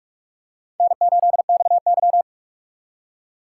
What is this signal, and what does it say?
Morse code, heard as a keyed tone, about 46 wpm, 700 Hz. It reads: N8XY
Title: USA Call Signs - 4 Characters - 4x Character Spacing - 45wpm